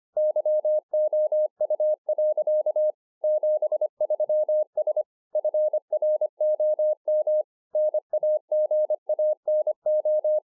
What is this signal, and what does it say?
Morse code, heard as a keyed tone, about 25 wpm, 610 Hz. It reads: YOU. 73S FROM NAGANO